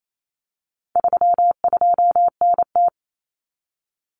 Morse code, heard as a keyed tone, about 28 wpm, 700 Hz. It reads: 32NT
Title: Common Cut Numbers - 4-Digit 28wpm